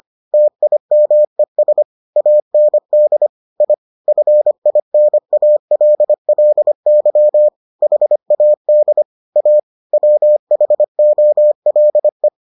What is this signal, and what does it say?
Morse code, heard as a keyed tone, about 25 wpm, 600 Hz. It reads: TIMES AND I FINALLY HAD A WHOLE